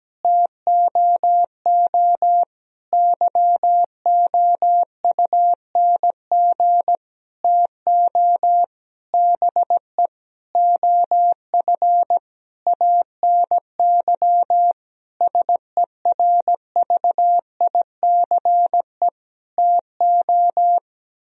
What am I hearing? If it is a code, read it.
Morse code, heard as a keyed tone, about 17 wpm, 700 Hz. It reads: TOO YOUNG TO BE OF ANY SERVICE TO
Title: Sentences from Top 1000 Words - Review Part 4 - 17wpm